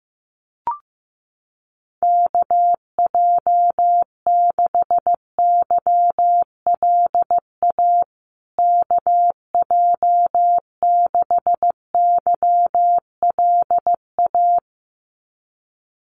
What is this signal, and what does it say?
Morse code, heard as a keyed tone, about 15 wpm, 700 Hz. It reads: KJ6YLA KJ6YLA
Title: USA Call Signs Repeated 2x 15wpm